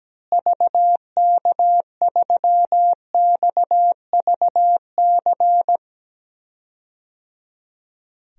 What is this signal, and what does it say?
Morse code, heard as a keyed tone, about 17 wpm, 700 Hz. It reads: VK3XVC
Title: DXCC-100 Call Signs - 6 Characters - Repeated 3x - 17wpm